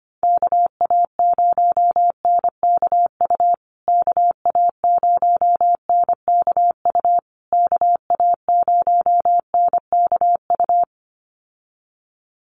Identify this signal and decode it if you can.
Morse code, heard as a keyed tone, about 25 wpm, 700 Hz. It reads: KA0NKU KA0NKU KA0NKU
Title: USA Call Signs Repeated 3x 25wpm